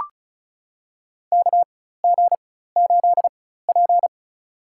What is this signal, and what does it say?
Morse code, heard as a keyed tone, about 34 wpm, spaced as for 16 wpm, 700 Hz. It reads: KG8P
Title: USA Call Signs - 4 Characters - 4x Character Spacing - 35wpm